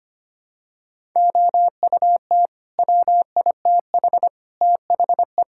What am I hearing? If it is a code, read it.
Morse code, heard as a keyed tone, about 25 wpm, 700 Hz. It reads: OUT WITH THE
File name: Sets of 9 Words - 25wpm